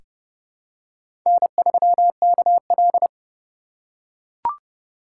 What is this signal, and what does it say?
Morse code, heard as a keyed tone, about 30 wpm, 700 Hz. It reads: N3KL
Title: USA Call Signs - 4 Characters - 4x Character Spacing - 30wpm